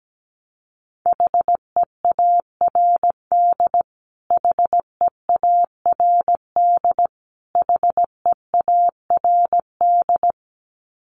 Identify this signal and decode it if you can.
Morse code, heard as a keyed tone, about 17 wpm, 700 Hz. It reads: HEARD HEARD HEARD
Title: Top 500 Words Repeated 17wpm